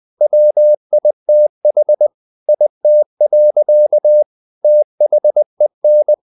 Morse code, heard as a keyed tone, about 20 wpm, 600 Hz. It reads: WITH IT. THEN
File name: the old man and the sea / the old man and the sea part 1 20wpm